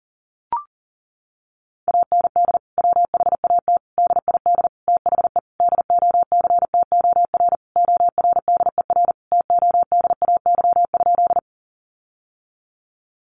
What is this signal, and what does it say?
Morse code, heard as a keyed tone, about 37 wpm, 700 Hz. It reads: AND WHAT DID THE DOCTOR ORDER TODAY?
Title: Sentences from Top 1500 Words 40wpm